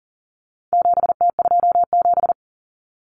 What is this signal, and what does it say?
Morse code, heard as a keyed tone, about 40 wpm, 700 Hz. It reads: 7T27
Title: Common Cut Numbers - 4-Digit 40wpm